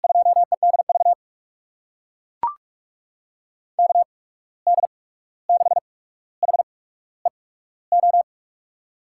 Morse code, heard as a keyed tone, about 44 wpm, spaced as for 17 wpm, 700 Hz. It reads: 2EDV KD6HEO
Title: USA Call Signs - 6 Characters - 8x Character Spacing - 45wpm